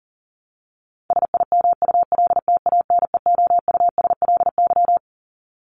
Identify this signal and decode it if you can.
Morse code, heard as a keyed tone, about 40 wpm, 700 Hz. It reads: SIMULTANEOUSLY
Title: LY Suffix - Top 100 Words with Base Words 40wpm